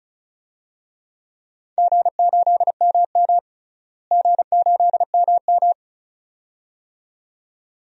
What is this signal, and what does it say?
Morse code, heard as a keyed tone, about 36 wpm, 700 Hz. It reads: G8MM G8MM
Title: DXCC-100 Call Signs - 4 Characters - Repeated 2x - 35wpm